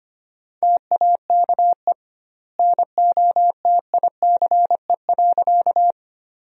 Morse code, heard as a keyed tone, about 25 wpm, 700 Hz. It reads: TAKE NOTICE.
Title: 2x Word-Spacing - Sentences from Top 400 Words - 25wpm